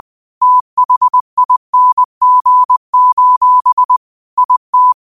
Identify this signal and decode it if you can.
Morse code, heard as a keyed tone, about 20 wpm, 1000 Hz. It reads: THING: IT
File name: qotdfiles / 20wpm